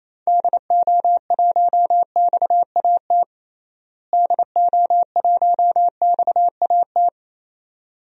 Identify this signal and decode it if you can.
Morse code, heard as a keyed tone, about 28 wpm, 700 Hz. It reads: DO1XAT DO1XAT
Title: DXCC-100 Call Signs - 6 Characters - Repeated 2x - 28wpm